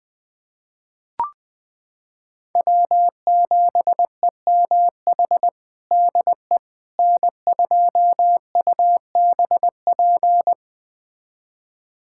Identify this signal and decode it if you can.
Morse code, heard as a keyed tone, about 21 wpm, 700 Hz. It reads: W7EMH DE N2UBP
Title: USA Callsign DE Callsign - 5 Characters - 20wpm